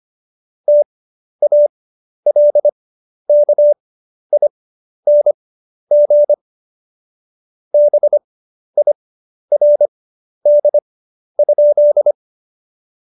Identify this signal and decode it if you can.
Morse code, heard as a keyed tone, about 25 wpm, spaced as for 11 wpm, 590 Hz. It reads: TALKING BIRD?
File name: Jokes-25-char-10-eff